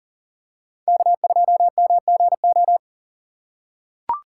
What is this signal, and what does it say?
Morse code, heard as a keyed tone, about 40 wpm, 700 Hz. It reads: K2MGO
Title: USA Call Signs - 5 Characters - 4x Character Spacing - 40wpm